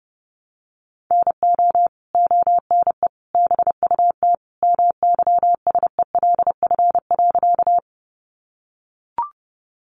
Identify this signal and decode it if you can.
Morse code, heard as a keyed tone, about 30 wpm, 700 Hz. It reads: NO ONE BUT MYSELF.
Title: Sentences from Top 1300 Words 30wpm